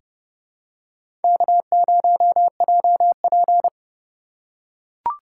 Morse code, heard as a keyed tone, about 31 wpm, 700 Hz. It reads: K0JP
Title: USA Call Signs - 4 Characters - 3x Character Spacing - 30wpm